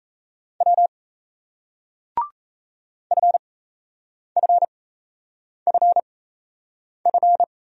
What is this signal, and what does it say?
Morse code, heard as a keyed tone, about 40 wpm, spaced as for 8 wpm, 700 Hz. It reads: W FFFF